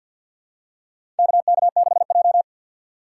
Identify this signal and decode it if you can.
Morse code, heard as a keyed tone, about 49 wpm, 700 Hz. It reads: KK6J
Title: USA Call Signs - 4 Characters - 6x Character Spacing - 50wpm